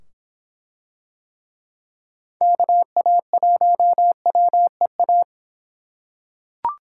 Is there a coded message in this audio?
Yes: Morse code from 2.4 to 5.2 s, heard as a keyed tone, about 24 wpm, 700 Hz. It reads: KA1WEA